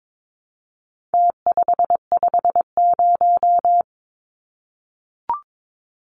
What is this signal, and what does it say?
Morse code, heard as a keyed tone, about 22 wpm, 700 Hz. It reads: T550
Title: Numbers - Contest Style - 4-Digit 22wpm